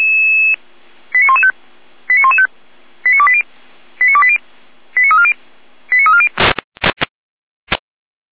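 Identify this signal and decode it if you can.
Five-tone selective call (ZVEI), heard as quick runs of five tones, tones of about 70 ms. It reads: 89115 89115 89280 89280 89340 89340
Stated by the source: ZVEI five-tone selective call received on SDR.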